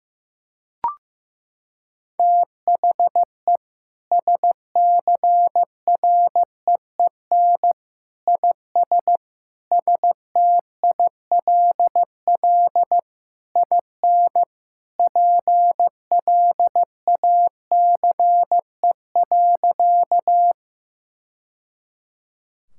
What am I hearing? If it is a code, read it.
Morse code, heard as a keyed tone, about 15 wpm, 700 Hz. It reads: THE SCREEN IS STILL IN PLACE.
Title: Sentences from Top 1000 Words - Review Part 6 - 15wpm